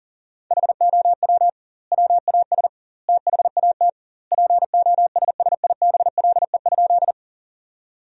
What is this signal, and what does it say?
Morse code, heard as a keyed tone, about 40 wpm, 700 Hz. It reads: HOW WAS THAT POSSIBLE?
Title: 2x Word-Spacing - Sentences from Top 500 Words - 40wpm